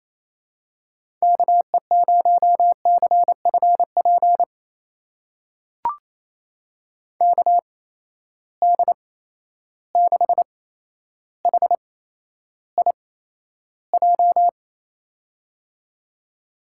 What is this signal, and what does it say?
Morse code, heard as a keyed tone, about 28 wpm, spaced as for 12 wpm, 700 Hz. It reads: KE0CFP KD6HIJ
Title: USA Call Signs - 8x Character Spacing - 28wpm